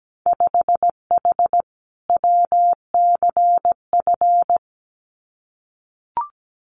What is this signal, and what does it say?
Morse code, heard as a keyed tone, about 17 wpm, 700 Hz. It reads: 5H WCF
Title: Winter Field Day Exchanges Encore 17wpm